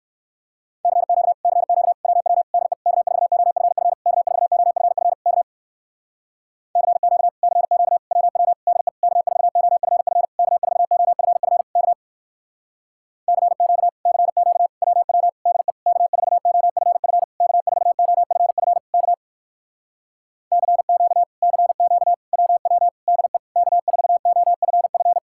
Morse code, heard as a keyed tone, about 65 wpm, 700 Hz. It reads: CQ CQ WW DE K4OFF K4OFF K CQ CQ WW DE K4OFF K4OFF K CQ CQ WW DE K4OFF K4OFF K CQ CQ WW DE K4OFF